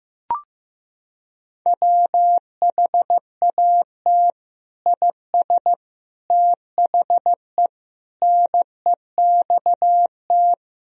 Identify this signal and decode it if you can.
Morse code, heard as a keyed tone, about 16 wpm, 700 Hz. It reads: WHAT IS THE NEXT
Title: Sentences from Top 1000 Words - Review Part 1 - 15wpm